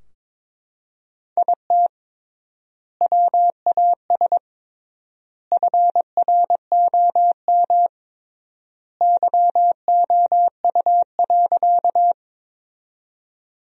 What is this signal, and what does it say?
Morse code, heard as a keyed tone, about 22 wpm, 700 Hz. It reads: IT WAS FROM YOU.
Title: 3x Word-Spacing - Sentences from Top 100 Words - 22wpm